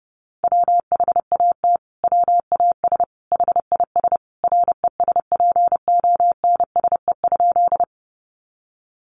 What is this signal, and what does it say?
Morse code, heard as a keyed tone, about 30 wpm, 700 Hz. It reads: WHAT WAS HIS RESPONSE?
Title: Sentences from Top 1400 Words 30wpm